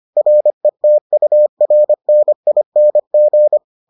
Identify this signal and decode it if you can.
Morse code, heard as a keyed tone, about 25 wpm, 600 Hz. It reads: RETURNING